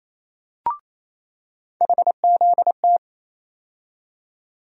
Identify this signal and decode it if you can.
Morse code, heard as a keyed tone, about 28 wpm, 700 Hz. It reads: HZT
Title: USA Call Sign Postfixes - 3 Letters 28wpm